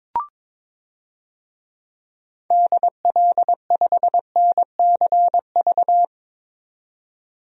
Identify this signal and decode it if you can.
Morse code, heard as a keyed tone, about 22 wpm, 700 Hz. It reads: DL5NCV
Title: DXCC-100 Call Signs - 6 Characters - 22wpm